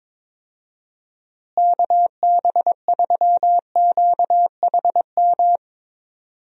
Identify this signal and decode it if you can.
Morse code, heard as a keyed tone, about 22 wpm, 700 Hz. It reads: KB3QHM
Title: USA Call Signs - 6 Characters - Repeated 3x - 22wpm